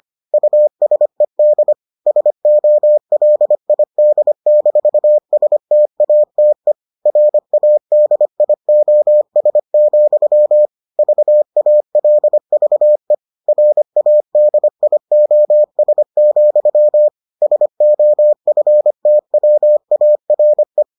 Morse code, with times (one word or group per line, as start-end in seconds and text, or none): 0.3-1.7 USED
2.1-6.7 SOLID-STATE
7.1-10.7 RADIOS,
11.0-13.1 VALVE
13.5-17.1 RADIOS,
17.4-20.8 SOFTWARE